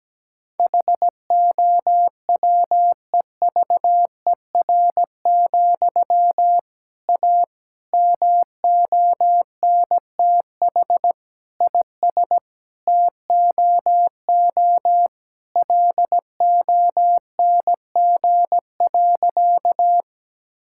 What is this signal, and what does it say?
Morse code, heard as a keyed tone, about 17 wpm, 700 Hz. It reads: HOWEVER, A MONTH IS TOO LONG.
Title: Sentences from Top 1000 Words - Review Part 5 - 17wpm